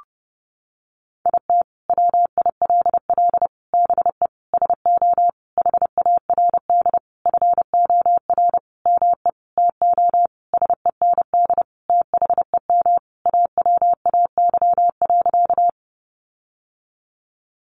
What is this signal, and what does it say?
Morse code, heard as a keyed tone, about 28 wpm, 700 Hz. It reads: IT WILL BE SO HARD FOR ME TO SEND THEM AWAY.
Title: Sentences from Top 1500 Words 30wpm